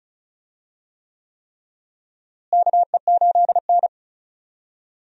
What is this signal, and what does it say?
Morse code, heard as a keyed tone, about 36 wpm, 700 Hz. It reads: KE8N
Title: USA Call Signs - 4 Characters - Repeated 3x - 35wpm